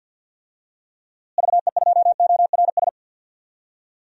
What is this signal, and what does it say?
Morse code, heard as a keyed tone, about 52 wpm, 700 Hz. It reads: VE2ORS
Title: DXCC-100 Call Signs - 6 Characters - Repeated 3x - 50wpm